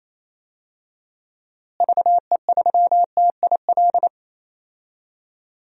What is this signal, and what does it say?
Morse code, heard as a keyed tone, about 28 wpm, 700 Hz. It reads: VE3TIL
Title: DXCC-100 Call Signs - 6 Characters - Repeated 3x - 28wpm